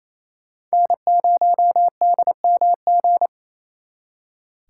Morse code, heard as a keyed tone, about 34 wpm, 700 Hz. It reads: N0DMG
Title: Speed Racing - USA Call Signs 28wpm